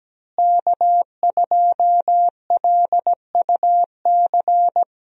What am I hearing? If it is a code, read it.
Morse code, heard as a keyed tone, about 17 wpm, 700 Hz. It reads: K2LUC
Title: USA Call Signs - 5 Characters - Repeated 3x - 17wpm